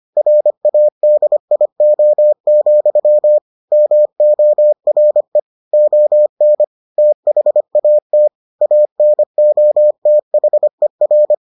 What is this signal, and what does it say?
Morse code, heard as a keyed tone, about 25 wpm, 600 Hz. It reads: RADIO, MORE ON THAT ANOTHER